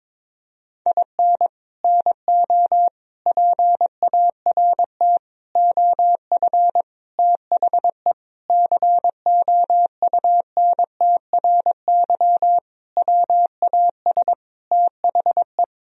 Morse code, heard as a keyed tone, about 22 wpm, 700 Hz. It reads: IN NO PART OF THE COUNTRY WAS THE
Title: Sentences from Top 1000 Words - Review Part 3 - 22wpm